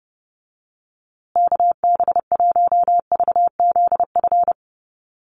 Morse code, heard as a keyed tone, about 30 wpm, 700 Hz. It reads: KB1VZF
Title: USA Call Signs Repeated 2x 30wpm